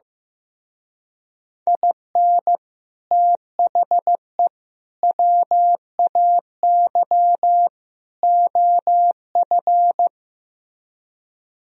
Morse code, heard as a keyed tone, about 16 wpm, 700 Hz. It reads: IN THE WAY OF
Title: Speed-Racing - Sets of 4 Words 15wpm